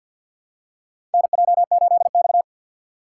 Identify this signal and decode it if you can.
Morse code, heard as a keyed tone, about 50 wpm, 700 Hz. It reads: NJ8X
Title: USA Call Signs - 4x Character Spacing - 50wpm